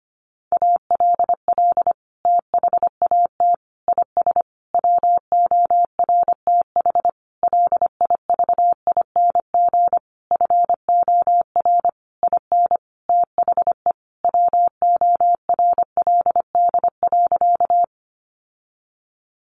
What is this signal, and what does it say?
Morse code, heard as a keyed tone, about 25 wpm, 700 Hz. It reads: ALL THAT IS WORTH LIVING FOR IN THE WORLD.
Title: Sentences from Top 1100 Words 25wpm